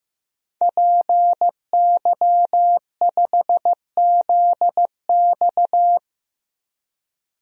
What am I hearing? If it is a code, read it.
Morse code, heard as a keyed tone, about 16 wpm, 700 Hz. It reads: PY5ZX
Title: DXCC-100 Call Signs - 5 Characters - 15wpm